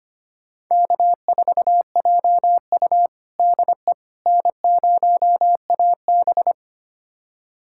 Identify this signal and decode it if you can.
Morse code, heard as a keyed tone, about 25 wpm, 700 Hz. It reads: K4JU DE N0AB
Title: USA Callsign DE Callsign - 4 Characters - 25wpm